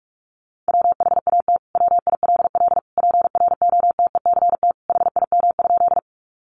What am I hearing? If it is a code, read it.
Morse code, heard as a keyed tone, about 44 wpm, 700 Hz. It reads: WHAT WILL PROTECT HIM?
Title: Sentences from Top 900 Words 45wpm